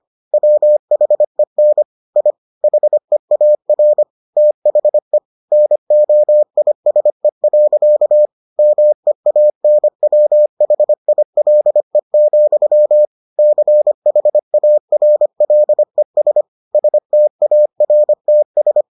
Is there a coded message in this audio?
Yes: Morse code from 0.3 to 18.8 s, heard as a keyed tone, about 25 wpm, 600 Hz. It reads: WHEN I HEAR THE NOISE. MEANWHILE, CHARLES STARTS